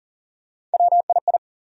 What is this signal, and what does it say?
Morse code, heard as a keyed tone, about 40 wpm, 700 Hz. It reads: WII
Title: USA Call Sign Postfixes - 3 Letters 40wpm